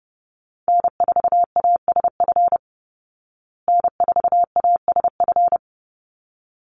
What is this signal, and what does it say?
Morse code, heard as a keyed tone, about 30 wpm, 700 Hz. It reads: N4ASF N4ASF
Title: Call Signs Repeated 3x - 30wpm